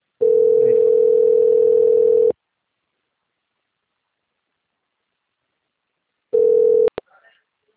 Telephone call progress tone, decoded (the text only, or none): ringback tone